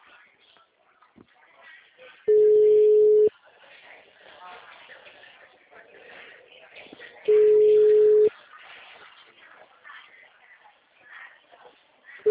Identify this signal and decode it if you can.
Ringback tone (Europe, 425 Hz), with speech.